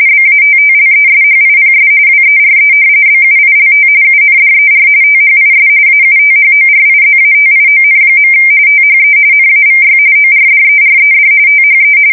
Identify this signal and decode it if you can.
RTTY, about 45.45 baud. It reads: BATTLESHIPS WERE BUILT UNDER THIS AUTHORIZATION. THE NAVY ORDERED THE
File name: fsk170 / battleshipnj